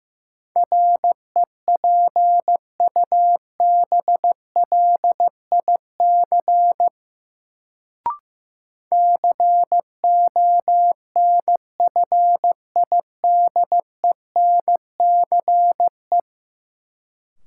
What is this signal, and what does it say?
Morse code, heard as a keyed tone, about 15 wpm, 700 Hz. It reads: REPUBLIC CONFIDENCE
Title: Sentences from Top 1500 Words - New Word Drill - 15wpm